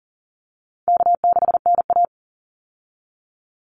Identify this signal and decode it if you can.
Morse code, heard as a keyed tone, about 38 wpm, 700 Hz. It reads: K6NA
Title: Easy Call Signs Repeated 3x - 40wpm